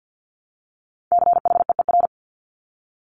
Morse code, heard as a keyed tone, about 50 wpm, 700 Hz. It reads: CHEER
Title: Multi-letter Phonemes Repeated 3x 50wpm